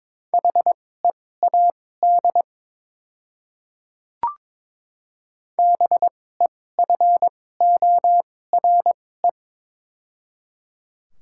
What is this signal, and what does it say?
Morse code, heard as a keyed tone, about 22 wpm, spaced as for 16 wpm, 700 Hz. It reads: HEAD BEFORE
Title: 2x Character Spacing - Top 500 Words - 22wpm